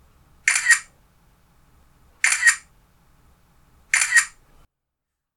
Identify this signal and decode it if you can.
There is no code.